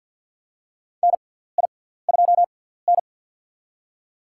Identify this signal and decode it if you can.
Morse code, heard as a keyed tone, about 51 wpm, spaced as for 17 wpm, 700 Hz. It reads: NI2N
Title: USA Call Signs - 4 Characters - 6x Character Spacing - 50wpm